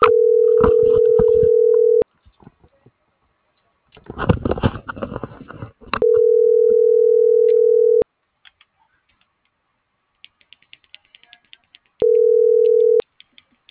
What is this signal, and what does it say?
Ringback tone (North America), 440 + 480 Hz.